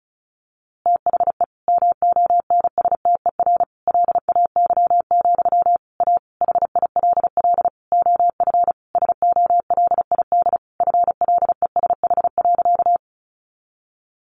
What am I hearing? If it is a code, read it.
Morse code, heard as a keyed tone, about 34 wpm, 700 Hz. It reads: THE MONSTER LAY, A HILL OF SOLID FLESH.
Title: A Sound Of Thunder 35wpm